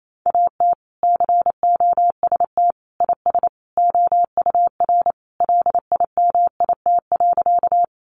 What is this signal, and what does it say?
Morse code, heard as a keyed tone, about 28 wpm, 700 Hz. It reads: AT COST IS OUR LIMIT.
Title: Sentences from Top 1500 Words 28wpm